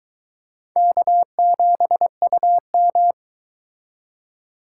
Morse code, heard as a keyed tone, about 23 wpm, 700 Hz. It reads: K7UM